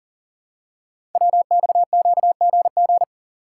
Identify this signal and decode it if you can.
Morse code, heard as a keyed tone, about 41 wpm, 690 Hz. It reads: WXQGG
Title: Code Groups - 5 Letters - 40wpm